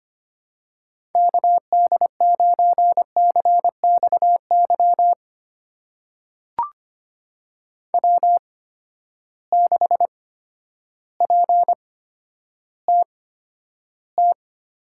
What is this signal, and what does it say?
Morse code, heard as a keyed tone, about 25 wpm, spaced as for 11 wpm, 700 Hz. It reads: KD9CXY W6PTT
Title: USA Call Signs - 8x Character Spacing - 25wpm